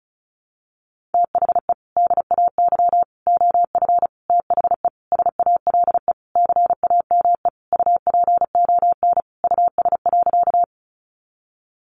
Morse code, heard as a keyed tone, about 34 wpm, 700 Hz. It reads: THE DAY OF THE SALE CAME UPON US.
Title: Sentences from Top 1400 Words 35wpm